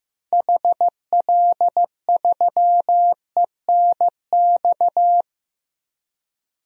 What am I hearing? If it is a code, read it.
Morse code, heard as a keyed tone, about 15 wpm, 700 Hz. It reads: HL3ENX
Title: DXCC-100 Call Signs - 6 Characters - 15wpm